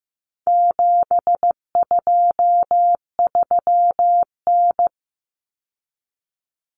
Morse code, heard as a keyed tone, about 15 wpm, 700 Hz. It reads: 723N
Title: Common Cut Numbers - 4-Digit 15wpm